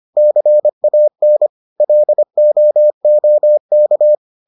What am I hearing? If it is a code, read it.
Morse code, heard as a keyed tone, about 25 wpm, 600 Hz. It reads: CAN LOOK